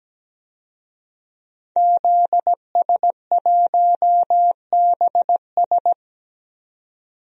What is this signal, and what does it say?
Morse code, heard as a keyed tone, about 17 wpm, 700 Hz. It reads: ZS1BS
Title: DXCC-100 Call Signs - 5 Characters - 17wpm